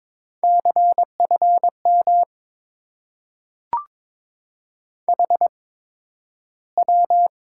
Morse code, heard as a keyed tone, about 23 wpm, spaced as for 12 wpm, 700 Hz. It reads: CFM HW
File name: Top 90 QSO Elements - 8x Character Spacing - 22wpm